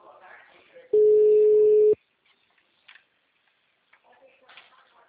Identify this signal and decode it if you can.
Ringback tone (Europe, 425 Hz), with speech.
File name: out- -2007- - - .68310